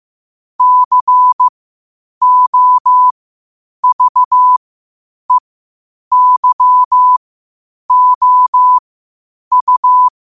Morse code, heard as a keyed tone, about 15 wpm, spaced as for 8 wpm, 1000 Hz. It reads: COVEYOU
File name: qotdfiles / 8wpm